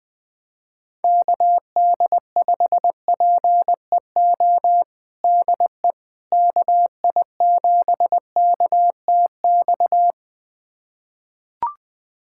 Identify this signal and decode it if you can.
Morse code, heard as a keyed tone, about 20 wpm, 700 Hz. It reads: KD5PEO DE KI7KTX
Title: USA Callsign DE Callsign - 6 Characters - 20wpm